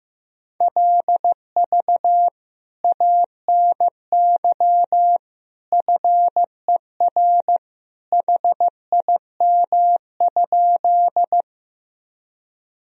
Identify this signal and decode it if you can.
Morse code, heard as a keyed tone, about 15 wpm, 700 Hz. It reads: LV ANY FER HIM?
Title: Sentences from Top 400 Words and Abbreviations 15wpm